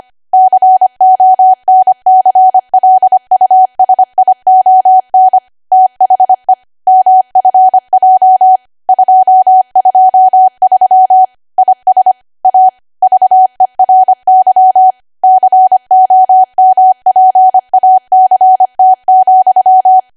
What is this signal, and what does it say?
Morse code, heard as a keyed tone, about 25 wpm, 750 Hz. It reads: CONCLUSION THE MFJ 223 IS A VERY COMPACT,
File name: Archive / 25 WPM / 25WPM